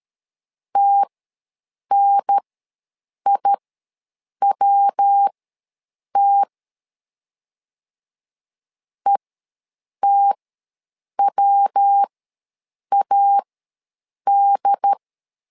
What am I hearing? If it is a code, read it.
Morse code, heard as a keyed tone, about 13 wpm, spaced as for 7 wpm, 800 Hz. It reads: TNIWT ETWAD
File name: LEZIONI / 3 MWRO / 3 MW ws 12wpm 1816